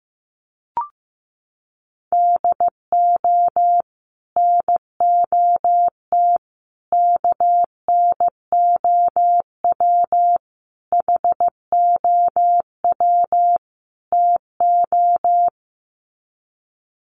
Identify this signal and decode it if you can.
Morse code, heard as a keyed tone, about 15 wpm, 700 Hz. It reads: DO NOT KNOW HOW TO
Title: Sets of 5 Words 15wpm